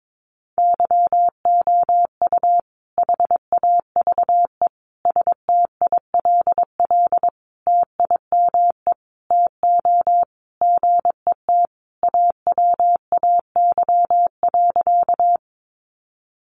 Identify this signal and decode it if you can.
Morse code, heard as a keyed tone, about 22 wpm, 700 Hz. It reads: YOU HAVE STILL TIME TO GET AWAY.
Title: Sentences from Top 600 Words 22wpm